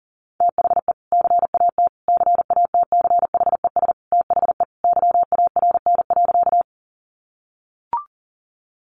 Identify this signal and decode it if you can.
Morse code, heard as a keyed tone, about 40 wpm, 700 Hz. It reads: THE CAT CATCHES THE YARN.
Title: Additional Easy Phrases 40wpm